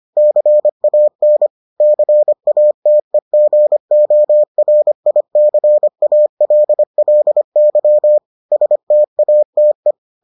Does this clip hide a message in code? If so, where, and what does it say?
Yes: Morse code from 0.2 to 9.9 s, heard as a keyed tone, about 25 wpm, 600 Hz. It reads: CAN CATEGORICALLY STATE